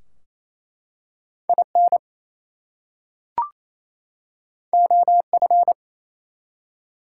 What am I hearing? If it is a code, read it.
Morse code, heard as a keyed tone, about 27 wpm, 700 Hz. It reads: IN OF
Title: Two Letter Words - 28wpm